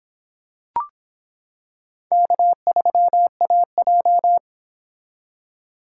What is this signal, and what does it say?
Morse code, heard as a keyed tone, about 25 wpm, 700 Hz. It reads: K3AJ